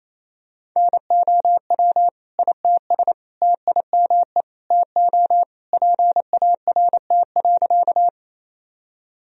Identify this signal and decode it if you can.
Morse code, heard as a keyed tone, about 28 wpm, 700 Hz. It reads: NOW ITS TIME TO PART.
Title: Sentences from Top 1000 Words - Review Part 1 - 28wpm